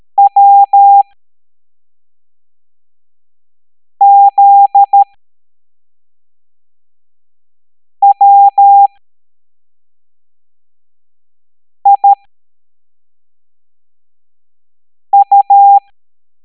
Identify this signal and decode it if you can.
Morse code, heard as a keyed tone, about 13 wpm, spaced as for 3 wpm, 800 Hz. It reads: WZWIU